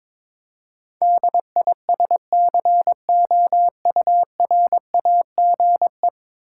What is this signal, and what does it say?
Morse code, heard as a keyed tone, about 22 wpm, 700 Hz. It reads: DISCOURAGE
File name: practice / 22wpm / Ten Letter Words - 22wpm